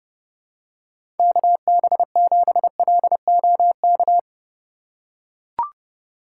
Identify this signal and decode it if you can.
Morse code, heard as a keyed tone, about 29 wpm, 700 Hz. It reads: KB7LOK
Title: USA Call Signs - 6 Characters - 4x Character Spacing - 30wpm